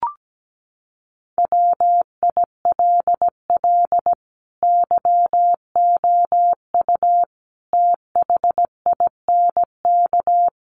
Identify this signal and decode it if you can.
Morse code, heard as a keyed tone, about 17 wpm, 700 Hz. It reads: WILL YOU THINK
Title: Sentences from Top 1200 Words 17wpm